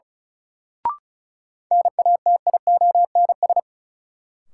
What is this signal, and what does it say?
Morse code, heard as a keyed tone, about 36 wpm, 700 Hz. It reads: NATIONS
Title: Sets of 3 Words - New Word Drill - 35wpm